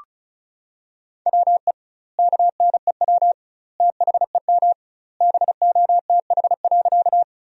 Morse code, heard as a keyed tone, about 34 wpm, 700 Hz. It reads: WE KNEW THEM BOTH.
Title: 2x Word-Spacing - Sentences from Top 400 Words - 35wpm